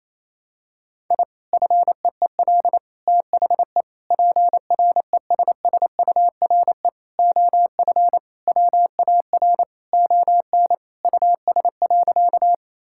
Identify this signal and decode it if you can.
Morse code, heard as a keyed tone, about 28 wpm, 700 Hz. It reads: I FEEL THE PRESSURE OF WAR ON US.
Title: Sentences from Top 1000 Words - Review Part 6 - 28wpm